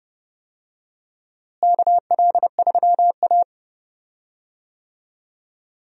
Morse code, heard as a keyed tone, about 30 wpm, 700 Hz. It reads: KL3A
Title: USA Call Signs - 4 Characters - Repeated 2x - 30wpm